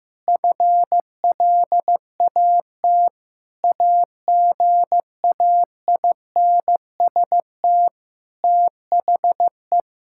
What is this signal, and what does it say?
Morse code, heard as a keyed tone, about 15 wpm, 700 Hz. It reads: FLAT AGAINST THE
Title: Sentences from Top 1600 Words 15wpm